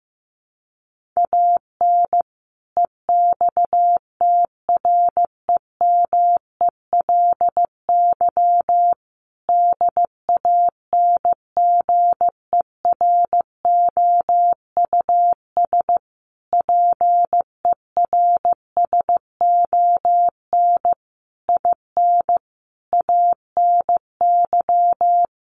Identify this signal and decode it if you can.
Morse code, heard as a keyed tone, about 15 wpm, 700 Hz. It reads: AN EXTREMELY DANGEROUS PERSON IN ANY
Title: Sentences from Top 1200 Words 15wpm